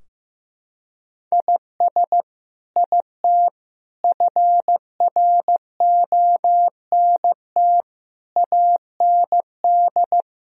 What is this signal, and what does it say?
Morse code, heard as a keyed tone, about 16 wpm, 700 Hz. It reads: IS IT FRONT AND
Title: Sentences from Top 1000 Words - Review Part 2 - 15wpm